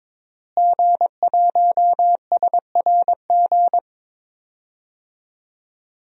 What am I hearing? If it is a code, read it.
Morse code, heard as a keyed tone, about 22 wpm, 700 Hz. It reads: G1SRG
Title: DXCC-100 Call Signs - 5 Characters - Repeated 2x - 22wpm